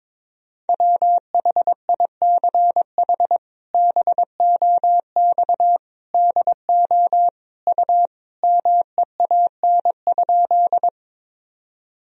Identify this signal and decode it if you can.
Morse code, heard as a keyed tone, about 22 wpm, 700 Hz. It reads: WHICH BOX DO U MEAN?
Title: Sentences from Top 500 Words and Abbreviations 22wpm